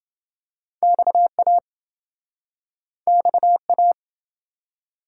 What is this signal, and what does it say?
Morse code, heard as a keyed tone, about 28 wpm, 700 Hz. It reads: XA XA